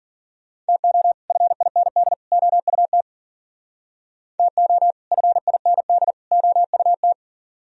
Morse code, heard as a keyed tone, about 50 wpm, 700 Hz. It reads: TO FIND OUT TO FIND OUT
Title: Speed-Racing - Sets of 3 Words 40wpm